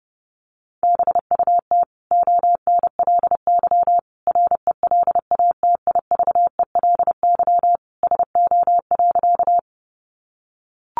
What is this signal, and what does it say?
Morse code, heard as a keyed tone, about 30 wpm, 700 Hz. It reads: BUT ONLY RELATIVELY SO.
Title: Sentences from Top 1100 Words 30wpm